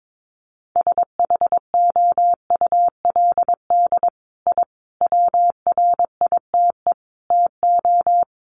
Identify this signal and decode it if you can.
Morse code, heard as a keyed tone, about 22 wpm, 700 Hz. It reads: SHOULD I WRITE TO
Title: Sentences from Top 1300 Words 22wpm